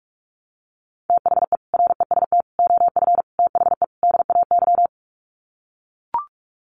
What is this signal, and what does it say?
Morse code, heard as a keyed tone, about 44 wpm, 700 Hz. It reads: THE REST OF THE DAY
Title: Sets of 5 Words 45wpm